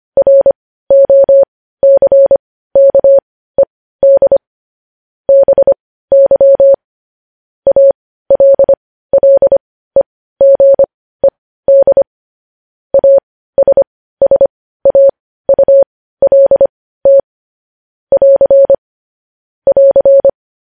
Morse code, heard as a keyed tone, about 25 wpm, spaced as for 15 wpm, 550 Hz. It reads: ROCKED BY ALLEGED ASSAULT <AR> <AR>
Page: news headlines in morse code at 15 wpm